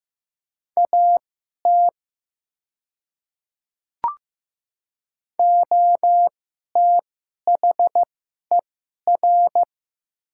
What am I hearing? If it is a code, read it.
Morse code, heard as a keyed tone, about 15 wpm, spaced as for 11 wpm, 700 Hz. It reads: AT OTHER
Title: 2x Character Spacing - Top 100 Words - 15wpm